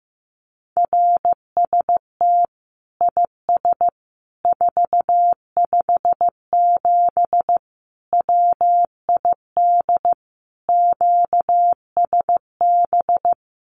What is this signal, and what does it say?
Morse code, heard as a keyed tone, about 15 wpm, 700 Hz. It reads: RST IS 457 WID QSB
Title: Common QSO Phrases 15wpm